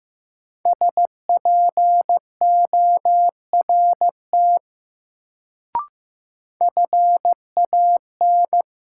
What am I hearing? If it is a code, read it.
Morse code, heard as a keyed tone, about 15 wpm, 700 Hz. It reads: SPORT FAN
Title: Sentences from Top 1000 Words - Review Part 4 - New Word Drill - 15wpm